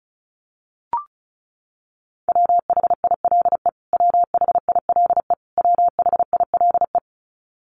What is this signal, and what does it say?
Morse code, heard as a keyed tone, about 34 wpm, 700 Hz. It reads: WHILE WHILE WHILE
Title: Top 500 Words Repeated 35wpm